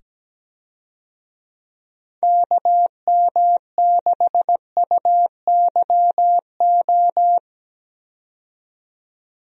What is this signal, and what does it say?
Morse code, heard as a keyed tone, about 17 wpm, 700 Hz. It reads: KM6UYO
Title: USA Call Signs - 6 Characters - 17wpm